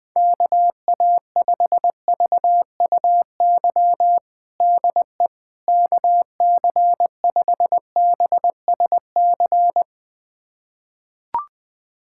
Morse code, heard as a keyed tone, about 21 wpm, 700 Hz. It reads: KA5VUY DE KC5BSC
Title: USA Callsign DE Callsign - 6 Characters - 20wpm